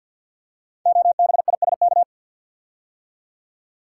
Morse code, heard as a keyed tone, about 52 wpm, 700 Hz. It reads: OBISK
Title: Code Groups - 5 Letters - 50wpm